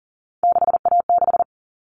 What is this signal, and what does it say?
Morse code, heard as a keyed tone, about 40 wpm, 700 Hz. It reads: 6A6
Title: All Cut Numbers - 3-Digit 40wpm